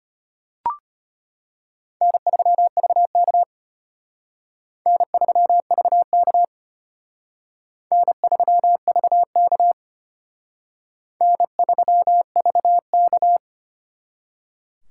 Morse code, heard as a keyed tone, about 32 wpm, 700 Hz. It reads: N3VK N3VK N3VK N3VK